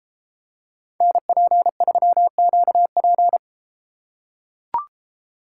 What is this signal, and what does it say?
Morse code, heard as a keyed tone, about 32 wpm, 700 Hz. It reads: NP3QP